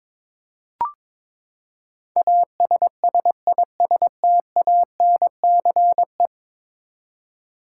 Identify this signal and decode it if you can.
Morse code, heard as a keyed tone, about 22 wpm, 700 Hz. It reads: ASSISTANCE
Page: morse-code-ninja-practice-22wpm